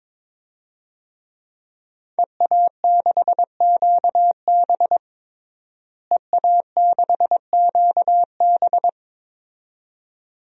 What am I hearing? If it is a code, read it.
Morse code, heard as a keyed tone, about 21 wpm, 700 Hz. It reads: EA6QB EA6QB
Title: DXCC-100 Call Signs - 5 Characters - Repeated 2x - 22wpm